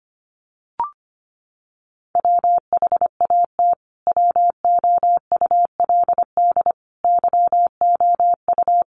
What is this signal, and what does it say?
Morse code, heard as a keyed tone, about 25 wpm, 700 Hz. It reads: WHAT WOULD YOU
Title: Sentences from Top 200 Words 25wpm